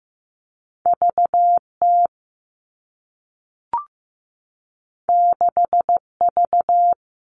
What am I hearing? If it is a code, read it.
Morse code, heard as a keyed tone, about 15 wpm, 700 Hz. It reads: VT 6V
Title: All Cut Numbers - 2-Digit 15wpm